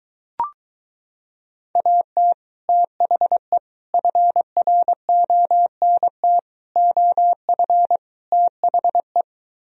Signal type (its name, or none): Morse code